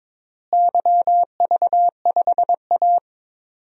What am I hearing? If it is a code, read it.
Morse code, heard as a keyed tone, about 22 wpm, 700 Hz. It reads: YV5A
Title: DXCC-100 Call Signs - 4 Characters - Repeated 2x - 22wpm